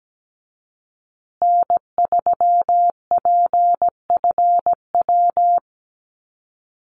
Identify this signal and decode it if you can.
Morse code, heard as a keyed tone, about 17 wpm, 700 Hz. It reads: N3PFW
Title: USA Call Signs Repeated 3x 17wpm